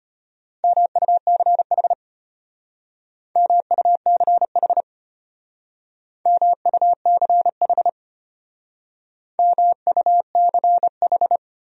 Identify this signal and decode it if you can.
Morse code, heard as a keyed tone, about 32 wpm, 700 Hz. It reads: MUCH MUCH MUCH MUCH